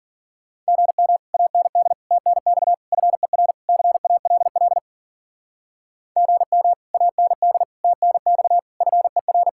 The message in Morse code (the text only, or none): GM AND TNX FER CALL GM AND TNX FER